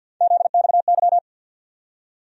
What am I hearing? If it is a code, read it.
Morse code, heard as a keyed tone, about 48 wpm, 700 Hz. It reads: ZXY
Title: USA Call Sign Postfixes - 3 Letters 50wpm